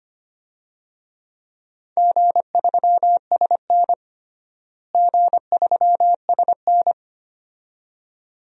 Morse code, heard as a keyed tone, about 25 wpm, 700 Hz. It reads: G3SN G3SN
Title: DXCC-100 Call Signs - 4 Characters - Repeated 2x - 25wpm